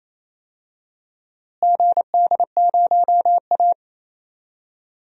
Morse code, heard as a keyed tone, about 28 wpm, 700 Hz. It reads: GD0A
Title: DXCC-100 Call Signs - 4 Characters - Repeated 2x - 28wpm